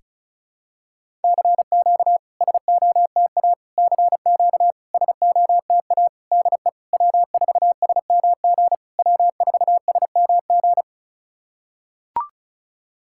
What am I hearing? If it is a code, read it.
Morse code, heard as a keyed tone, about 36 wpm, 690 Hz. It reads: CQ SOTA CQ SOTA DE W4SMG W4SMG
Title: SOTA QSO Practice 35wpm